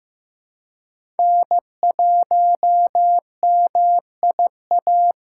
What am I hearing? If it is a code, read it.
Morse code, heard as a keyed tone, about 15 wpm, 700 Hz. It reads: N1MIA
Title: USA Call Signs - 5 Characters - 3x Character Spacing - 15wpm